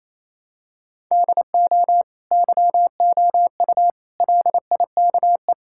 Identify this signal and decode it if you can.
Morse code, heard as a keyed tone, about 27 wpm, 700 Hz. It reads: DO YOU LIKE
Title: Sentences from Top 1000 Words - Review Part 6 - 28wpm